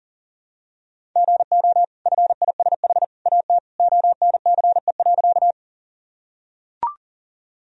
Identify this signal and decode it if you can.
Morse code, heard as a keyed tone, about 37 wpm, 700 Hz. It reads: GO FISH AT ONCE.
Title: Sentences from Top 1000 Words - Review Part 1 - 40wpm